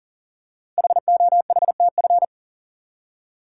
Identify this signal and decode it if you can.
Morse code, heard as a keyed tone, about 40 wpm, 700 Hz. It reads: HOHTF
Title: Code Groups - 5 Letters - 40wpm